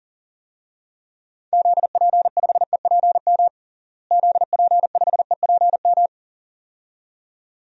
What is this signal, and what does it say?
Morse code, heard as a keyed tone, about 40 wpm, 700 Hz. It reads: ZP5EPM ZP5EPM
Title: DXCC-100 Call Signs - 6 Characters - Repeated 2x - 40wpm